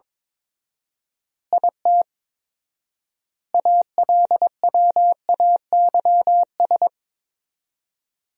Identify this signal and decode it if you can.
Morse code, heard as a keyed tone, about 22 wpm, 700 Hz. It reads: IT ALWAYS
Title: 4x Word-Spacing - Sentences from Top 300 Words - 22wpm